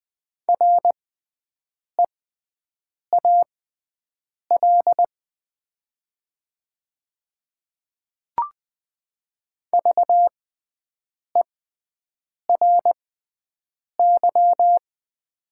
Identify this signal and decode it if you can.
Morse code, heard as a keyed tone, about 20 wpm, spaced as for 7 wpm, 700 Hz. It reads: REAL VERY
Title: 6x Character Spacing - Top 300 Words - 20wpm